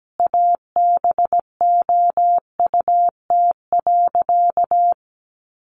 Morse code, heard as a keyed tone, about 17 wpm, 700 Hz. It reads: ABOUT.
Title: Sentences from Top 1100 Words 17wpm